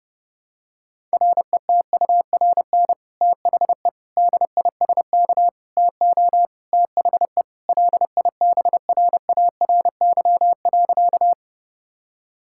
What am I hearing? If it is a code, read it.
Morse code, heard as a keyed tone, about 30 wpm, 700 Hz. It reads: RETURN THE DISK TO THE LIBRARY.
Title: Sentences from Top 1000 Words - Review Part 6 - 30wpm